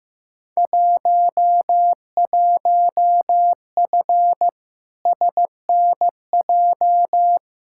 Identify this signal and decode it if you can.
Morse code, heard as a keyed tone, about 15 wpm, 700 Hz. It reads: 11F SNJ
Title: Field Day Exchanges Encore - 15wpm